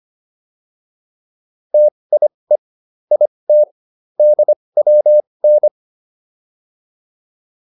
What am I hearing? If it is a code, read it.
Morse code, heard as a keyed tone, about 25 wpm, spaced as for 20 wpm, 600 Hz. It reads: TIE IT DWN